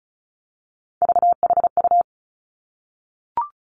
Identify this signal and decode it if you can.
Morse code, heard as a keyed tone, about 36 wpm, 700 Hz. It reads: VHU
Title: Three Challenging Letter-Number Combinations 35wpm